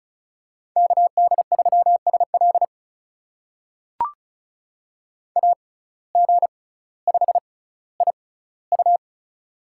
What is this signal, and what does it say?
Morse code, heard as a keyed tone, about 36 wpm, spaced as for 18 wpm, 700 Hz. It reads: KD3SL AG5IU
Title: USA Call Signs - 5 Characters - 6x Character Spacing - 35wpm